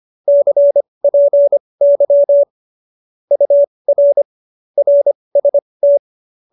Morse code, heard as a keyed tone, about 25 wpm, spaced as for 20 wpm, 570 Hz. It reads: CPY UR RST
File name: qso216 20